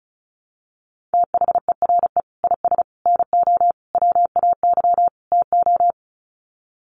Morse code, heard as a keyed tone, about 34 wpm, 700 Hz. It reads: THERE IS NO WAY TO
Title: Sets of 5 Words Encore 35wpm